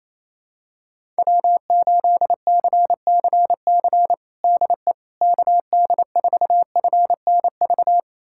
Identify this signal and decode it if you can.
Morse code, heard as a keyed tone, about 28 wpm, 700 Hz. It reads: W8CCC DE KD4FNV
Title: USA Callsign DE Callsign 28wpm